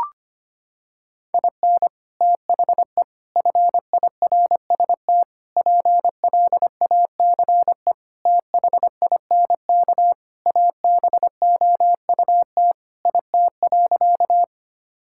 Morse code, heard as a keyed tone, about 25 wpm, 700 Hz. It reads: IN THE FIRST PLACE THINK ABOUT IT.
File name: practice / 25wpm / Sentences with 3-Word Phrases - 25wpm